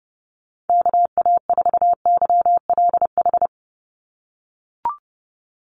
Morse code, heard as a keyed tone, about 30 wpm, 700 Hz. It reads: KA4YLH
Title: USA Call Signs 30wpm